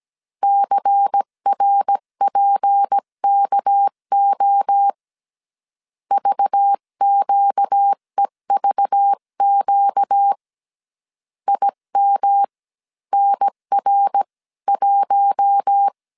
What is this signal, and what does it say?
Morse code, heard as a keyed tone, about 17 wpm, spaced as for 14 wpm, 800 Hz. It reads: CRPKO VQEVQ IMNR1
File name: CW / DET / 1 6 1500C S C 16wpm 2733